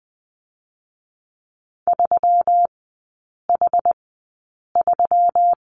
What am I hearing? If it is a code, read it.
Morse code, heard as a keyed tone, about 20 wpm, spaced as for 8 wpm, 700 Hz. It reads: 3H3
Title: 127 Lesson 21e - Number 6 - Three Characters 20wpm